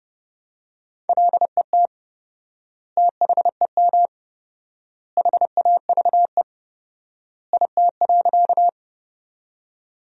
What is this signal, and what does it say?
Morse code, heard as a keyed tone, about 30 wpm, 700 Hz. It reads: LET THEM HAVE IT.
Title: 4x Word-Spacing - Sentences from Top 300 Words - 30wpm